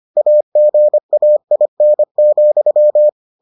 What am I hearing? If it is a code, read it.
Morse code, heard as a keyed tone, about 25 wpm, 600 Hz. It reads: AGAIN,